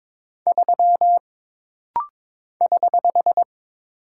Morse code, heard as a keyed tone, about 22 wpm, 700 Hz. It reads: 3 <HH>
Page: This rapid-fire timing is geared to push your ICR (Instant Character Recognition) skills.